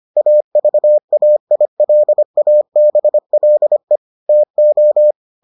Morse code, heard as a keyed tone, about 25 wpm, 600 Hz. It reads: AVAILABLE TO